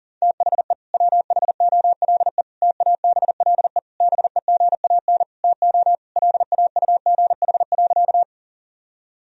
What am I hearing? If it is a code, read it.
Morse code, heard as a keyed tone, about 40 wpm, 700 Hz. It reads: THE WHOLE TABLE BEGAN TO LAUGH.
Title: Sentences from Top 1000 Words - Review Part 2 - 40wpm